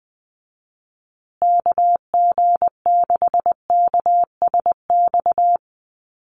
Morse code, heard as a keyed tone, about 20 wpm, 700 Hz. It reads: KG6KSX
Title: USA Call Signs Repeated 3x 20wpm